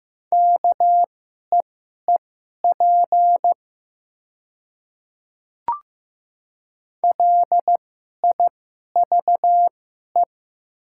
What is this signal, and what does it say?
Morse code, heard as a keyed tone, about 15 wpm, spaced as for 11 wpm, 700 Hz. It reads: KEEP LIVE
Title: 2x Character Spacing - Top 300 Words - 15wpm